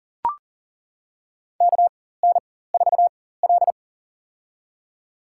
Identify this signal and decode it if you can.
Morse code, heard as a keyed tone, about 40 wpm, spaced as for 19 wpm, 700 Hz. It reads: KN4L
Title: USA Call Signs - 4x Character Spacing - 40wpm